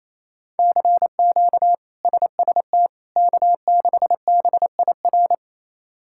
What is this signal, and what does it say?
Morse code, heard as a keyed tone, about 28 wpm, 700 Hz. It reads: CQ SST K6BIR
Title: K1USN SST QSO Practice - 28wpm